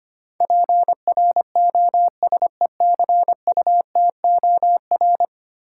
Morse code, heard as a keyed tone, about 25 wpm, 700 Hz. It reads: PROSECUTOR
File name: 10 Letter Words - First 3 Letters Given - 25wpm